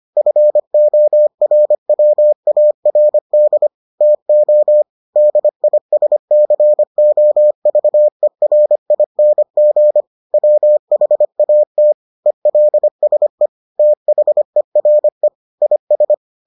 Morse code, heard as a keyed tone, about 25 wpm, 600 Hz. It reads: FORWARD TO DISCOVERING WHAT ELSE THERE IS